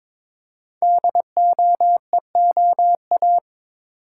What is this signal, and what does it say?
Morse code, heard as a keyed tone, about 22 wpm, 700 Hz. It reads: DOEOA
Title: Code Groups - 5 Letters - 22wpm